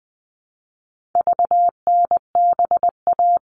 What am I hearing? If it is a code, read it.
Morse code, heard as a keyed tone, about 20 wpm, 700 Hz. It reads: VNBA
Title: All Cut Numbers - 4-Digit 20wpm